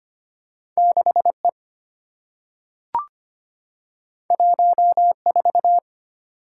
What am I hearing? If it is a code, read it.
Morse code, heard as a keyed tone, about 25 wpm, 700 Hz. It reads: 6E 14
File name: Two Letter-Number Combinations - 25wpm